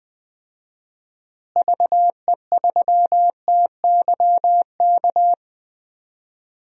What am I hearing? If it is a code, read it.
Morse code, heard as a keyed tone, about 21 wpm, 700 Hz. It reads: VE3TYK
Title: DXCC-100 Call Signs - 6 Characters - Repeated 3x - 20wpm